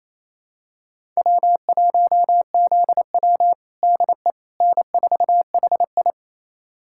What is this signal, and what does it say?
Morse code, heard as a keyed tone, about 28 wpm, 700 Hz. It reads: W1ZW DE N4HI
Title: USA Callsign DE Callsign - 4 Characters - 28wpm